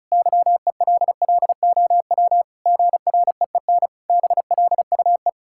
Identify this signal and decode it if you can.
Morse code, heard as a keyed tone, about 34 wpm, 700 Hz. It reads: YELLOW GREEN BLUE
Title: Pangram Challenge v2 - 35wpm